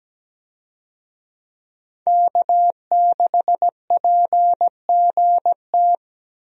Morse code, heard as a keyed tone, about 17 wpm, 700 Hz. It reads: K6PGT
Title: USA Call Signs - 5 Characters - Repeated 3x - 17wpm